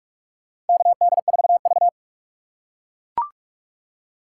Morse code, heard as a keyed tone, about 42 wpm, 700 Hz. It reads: KD4V
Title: USA Call Signs - 4 Characters - 2x Character Spacing - 45wpm